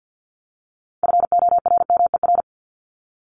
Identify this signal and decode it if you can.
Morse code, heard as a keyed tone, about 50 wpm, 700 Hz. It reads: FORMER
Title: LY Suffix - Top 100 Words with Base Words 50wpm